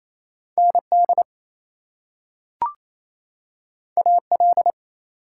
Morse code, heard as a keyed tone, about 28 wpm, 700 Hz. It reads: ND AL
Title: US States and Canadian Provinces 28wpm